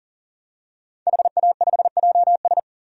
Morse code, heard as a keyed tone, about 40 wpm, 700 Hz. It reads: HA5JS
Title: DXCC-100 Call Signs - 5 Characters - 40wpm